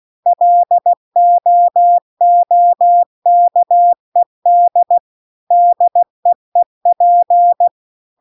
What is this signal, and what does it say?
Morse code, heard as a keyed tone, about 16 wpm, 700 Hz. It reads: LOOKED DEEP